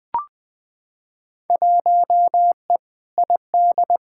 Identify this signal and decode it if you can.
Morse code, heard as a keyed tone, about 21 wpm, 700 Hz. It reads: 1E ID
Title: Field Day Exchanges - 20wpm